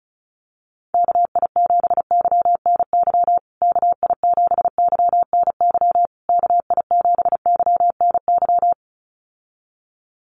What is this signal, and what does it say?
Morse code, heard as a keyed tone, about 34 wpm, 700 Hz. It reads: KI7YNY KI7YNY KI7YNY
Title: USA Call Signs Repeated 3x 35wpm